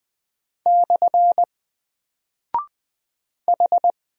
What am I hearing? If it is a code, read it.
Morse code, heard as a keyed tone, about 20 wpm, 700 Hz. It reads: / H
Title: Single Character (Letter:Number:Punctuation:Prosign) 20wpm